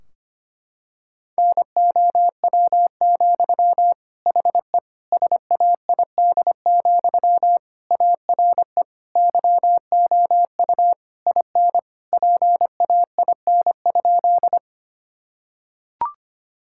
Morse code, heard as a keyed tone, about 25 wpm, 700 Hz. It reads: NOW, HE SAID, ARE YOU IN PAIN?
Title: Sentences from Top 1600 Words 25wpm